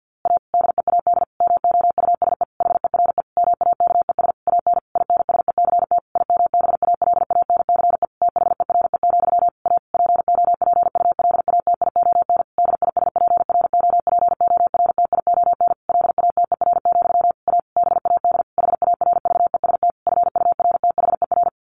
Morse code, heard as a keyed tone, about 48 wpm, 700 Hz. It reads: A DEAD MOUSE HERE MAKES AN INSECT IMBALANCE THERE, A POPULATION DISPROPORTION LATER, A BAD HARVEST FURTHER
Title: A Sound Of Thunder 50wpm